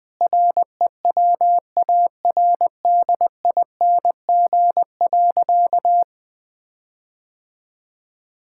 Morse code, heard as a keyed tone, about 20 wpm, 700 Hz. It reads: REWARDING.
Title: Sentences with 5-Word Phrases - Spoken Beginning - 20wpm